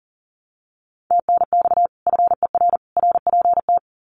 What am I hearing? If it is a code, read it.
Morse code, heard as a keyed tone, about 40 wpm, 700 Hz. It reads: TNX FER RPT
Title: Common QSO Phrases 40wpm